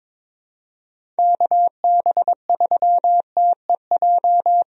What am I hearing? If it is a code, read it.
Morse code, heard as a keyed tone, about 22 wpm, 700 Hz. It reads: KB3TEJ
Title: USA Call Signs - 6 Characters - Repeated 2x - 22wpm